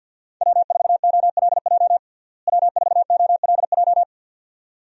Morse code, heard as a keyed tone, about 49 wpm, 700 Hz. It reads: W4OLJ W4OLJ
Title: USA Call Signs - 5 Characters - Repeated 2x - 50wpm